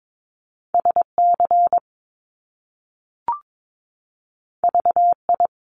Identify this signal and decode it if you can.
Morse code, heard as a keyed tone, about 22 wpm, 700 Hz. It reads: SC VI
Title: Top Two Letter Combinations in Words 22wpm